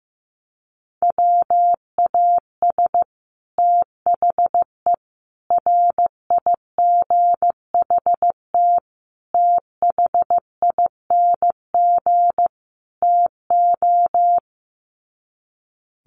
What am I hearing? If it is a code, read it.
Morse code, heard as a keyed tone, about 15 wpm, 700 Hz. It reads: WAS THE RIGHT THING TO
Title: Sets of 5 Words Encore2 15wpm